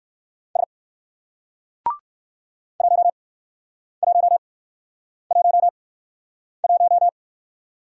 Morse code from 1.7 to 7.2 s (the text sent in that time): 1111